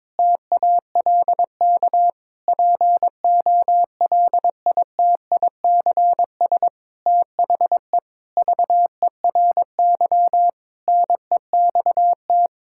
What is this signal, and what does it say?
Morse code, heard as a keyed tone, about 22 wpm, 700 Hz. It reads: TALK POLITICS THE VERY NEXT
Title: Sentences from Top 1000 Words - Review Part 6 - 22wpm